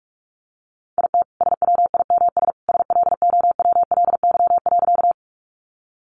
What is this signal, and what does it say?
Morse code, heard as a keyed tone, about 44 wpm, 700 Hz. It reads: IT SWIMS SLOWLY.
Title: Additional Easy Phrases 45wpm